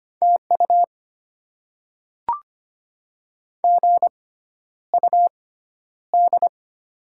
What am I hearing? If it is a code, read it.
Morse code, heard as a keyed tone, about 25 wpm, spaced as for 11 wpm, 700 Hz. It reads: TU GUD
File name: Top 20 QSO Elements - 6x Character Spacing - 25wpm